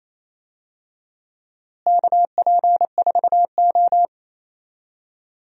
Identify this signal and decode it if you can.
Morse code, heard as a keyed tone, about 28 wpm, 700 Hz. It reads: KP4O
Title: DXCC-100 Call Signs - 4 Characters - Repeated 3x - 28wpm